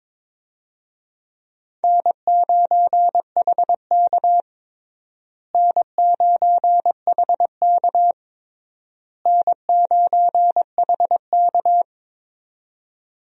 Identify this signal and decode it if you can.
Morse code, heard as a keyed tone, about 22 wpm, 700 Hz. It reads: N9HK N9HK N9HK
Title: USA Call Signs - 4 Characters - Repeated 3x - 22wpm